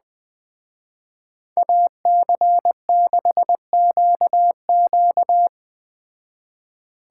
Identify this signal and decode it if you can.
Morse code, heard as a keyed tone, about 20 wpm, 700 Hz. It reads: AC6QQ
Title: Speed Racing - USA Call Signs 17wpm